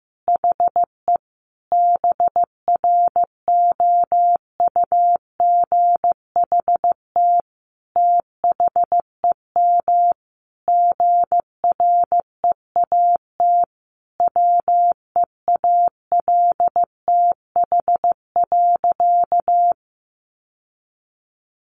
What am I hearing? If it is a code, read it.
Morse code, heard as a keyed tone, about 15 wpm, 700 Hz. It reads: HE BROUGHT THEM GREAT WEALTH.
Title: Sentences from Top 1300 Words 15wpm